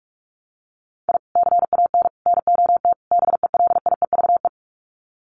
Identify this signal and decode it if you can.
Morse code, heard as a keyed tone, about 44 wpm, 700 Hz. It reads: I CAN NOT BELIEVE
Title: Sets of 4 Words 45wpm